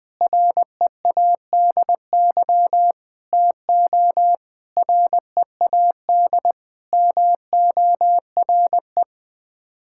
Morse code, heard as a keyed tone, about 20 wpm, 700 Hz. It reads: READY TO READ MORE
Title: Sets of 6 Words - 20wpm